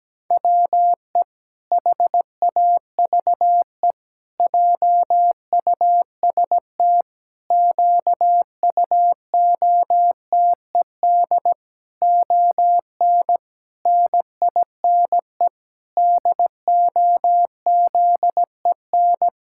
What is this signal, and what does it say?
Morse code, heard as a keyed tone, about 17 wpm, 700 Hz. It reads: WE HAVE JUST QUOTED ON NINE DOZEN
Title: Pangram Challenge v2 - 17wpm